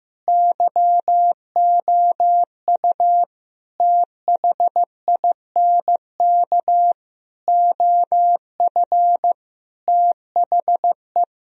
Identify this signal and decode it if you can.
Morse code, heard as a keyed tone, about 15 wpm, 700 Hz. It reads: YOU THINK OF THE
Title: Sentences from Top 1000 Words - Review Part 4 - 15wpm